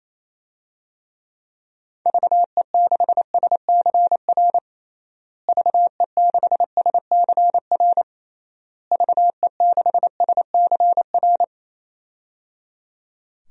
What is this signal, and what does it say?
Morse code, heard as a keyed tone, about 28 wpm, 700 Hz. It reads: VE6SCR VE6SCR VE6SCR
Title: DXCC-100 Call Signs - 6 Characters - Repeated 3x - 28wpm